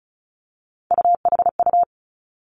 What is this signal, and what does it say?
Morse code, heard as a keyed tone, about 36 wpm, 700 Hz. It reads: UHU
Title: Three Challenging Letter-Number Combinations 35wpm